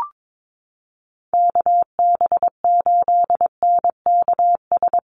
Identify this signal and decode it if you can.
Morse code, heard as a keyed tone, about 22 wpm, 690 Hz. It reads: KB8NKS
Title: USA Call Signs Repeated 3x 22wpm